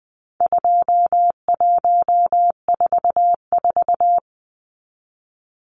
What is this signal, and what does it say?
Morse code, heard as a keyed tone, about 20 wpm, 700 Hz. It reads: 2144
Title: Numbers - 4-Digit 20wpm